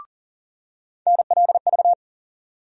Morse code, heard as a keyed tone, about 40 wpm, 700 Hz. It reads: NLV
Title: USA Call Sign Postfixes - 3 Letters 40wpm